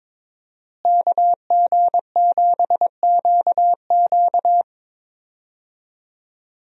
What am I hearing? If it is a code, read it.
Morse code, heard as a keyed tone, about 21 wpm, 700 Hz. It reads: KG7QQ
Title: USA Call Signs - 5 Characters - Repeated 2x - 22wpm